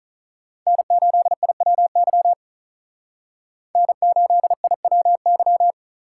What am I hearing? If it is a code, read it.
Morse code, heard as a keyed tone, about 46 wpm, 700 Hz. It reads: N8IWY N8IWY